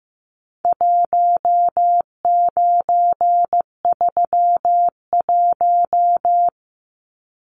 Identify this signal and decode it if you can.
Morse code, heard as a keyed tone, about 15 wpm, 700 Hz. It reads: 1931
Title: Numbers - 4-Digit 15wpm